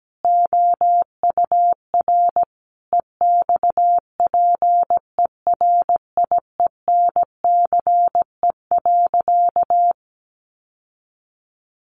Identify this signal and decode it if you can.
Morse code, heard as a keyed tone, about 17 wpm, 700 Hz. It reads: OUR EXPERIENCE.
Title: Sentences from Top 1200 Words 17wpm